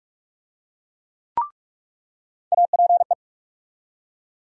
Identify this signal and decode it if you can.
Morse code, heard as a keyed tone, about 44 wpm, 700 Hz. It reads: APE
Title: USA Call Sign Postfixes - 3 Letters 45wpm